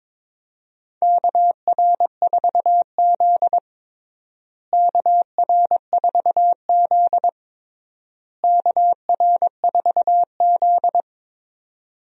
Morse code, heard as a keyed tone, about 22 wpm, 700 Hz. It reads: KR4Z KR4Z KR4Z
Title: USA Call Signs - 4 Characters - Repeated 3x - 22wpm